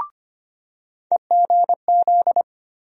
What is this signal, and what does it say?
Morse code, heard as a keyed tone, about 25 wpm, 700 Hz. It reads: EGZ
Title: USA Call Sign Postfixes - 3 Letters 25wpm